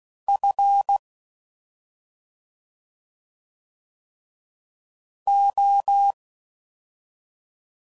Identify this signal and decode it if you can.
Morse code, heard as a keyed tone, about 16 wpm, 800 Hz. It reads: F O